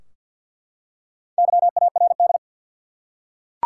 Morse code, heard as a keyed tone, about 49 wpm, 700 Hz. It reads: YARD
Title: Four Letter Words - 50wpm